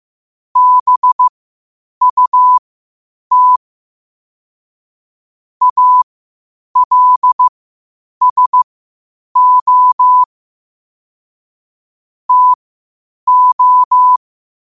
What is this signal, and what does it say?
Morse code, heard as a keyed tone, about 15 wpm, spaced as for 8 wpm, 1000 Hz. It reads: BUT ALSO TO
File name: qotdfiles / 8wpm